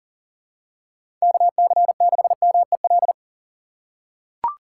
Morse code, heard as a keyed tone, about 40 wpm, 700 Hz. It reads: KC6MEL
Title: USA Call Signs - 6 Characters - 6x Character Spacing - 40wpm